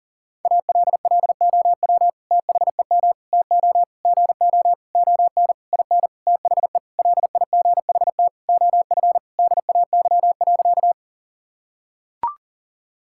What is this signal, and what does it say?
Morse code, heard as a keyed tone, about 37 wpm, 700 Hz. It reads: ALLOW THEM TO GO ON IN THE LIGHT OF DAY.
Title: Sentences from Top 1000 Words - Review Part 5 - 40wpm